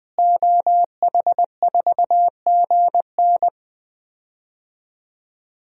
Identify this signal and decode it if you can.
Morse code, heard as a keyed tone, about 20 wpm, 700 Hz. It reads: OH4GN
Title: DXCC-100 Call Signs - 5 Characters - Repeated 3x - 20wpm